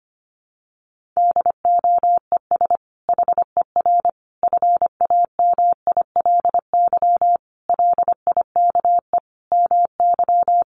Morse code, heard as a keyed tone, about 25 wpm, 700 Hz. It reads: DOES HER FAMILY LIKE MY
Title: Sentences from Top 1400 Words 25wpm